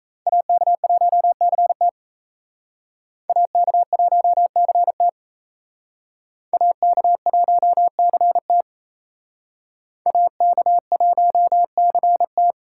Morse code, heard as a keyed tone, about 35 wpm, 700 Hz. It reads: AK1CT AK1CT AK1CT AK1CT
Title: Speed Racing - USA Call Signs 28wpm